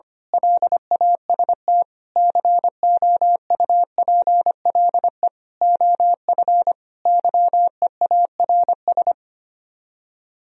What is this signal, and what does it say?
Morse code, heard as a keyed tone, about 24 wpm, 700 Hz. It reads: LAST COUPLE OF YEARS
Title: Speed-Racing - Sets of 5 Words 25wpm